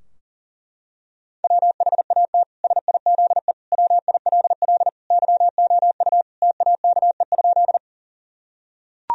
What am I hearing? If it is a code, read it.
Morse code, heard as a keyed tone, about 37 wpm, 700 Hz. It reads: WHAT SIZE WILL YOU TAKE?
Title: Sentences from Top 1000 Words - Review Part 2 - 40wpm